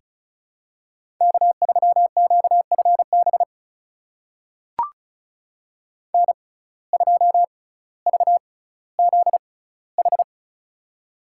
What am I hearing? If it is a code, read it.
Morse code, heard as a keyed tone, about 36 wpm, spaced as for 18 wpm, 700 Hz. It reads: K3QFB N2VZH
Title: USA Call Signs - 5 Characters - 6x Character Spacing - 35wpm